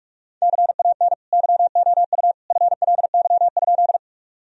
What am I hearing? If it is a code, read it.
Morse code, heard as a keyed tone, about 42 wpm, 700 Hz. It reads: CAN YOU FLY?
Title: Sentences from Top 1000 Words - Review Part 2 - 45wpm